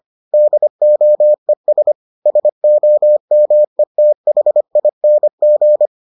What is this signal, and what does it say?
Morse code, heard as a keyed tone, about 25 wpm, 600 Hz. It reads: DOES SOMETHING